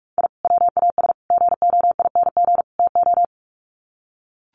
Morse code, heard as a keyed tone, about 44 wpm, 700 Hz. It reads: I WAS GOING TO
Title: Sets of 4 Words 45wpm